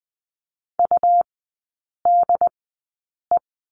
Morse code, heard as a keyed tone, about 20 wpm, spaced as for 8 wpm, 700 Hz. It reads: UDE
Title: 100 Lesson 17e - Letter u - Three Characters 20wpm